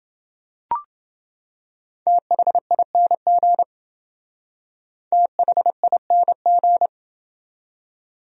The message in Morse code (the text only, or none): THING THING